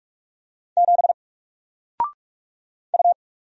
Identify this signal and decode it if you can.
Morse code, heard as a keyed tone, about 44 wpm, 700 Hz. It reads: 7 U
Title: Single Character (Letter:Number:Punctuation:Prosign) Alt 45wpm